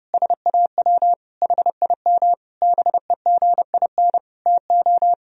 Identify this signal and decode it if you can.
Morse code, heard as a keyed tone, about 28 wpm, 700 Hz. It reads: SAW HIM BEGIN TO
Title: Sentences from Top 1000 Words - Review Part 2 - 30wpm